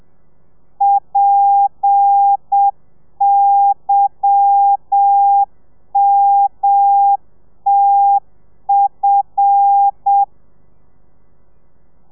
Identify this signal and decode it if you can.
Morse code, heard as a keyed tone, about 7 wpm, 800 Hz. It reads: PYMTF
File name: telegrafia / i2tfj / Lez10 30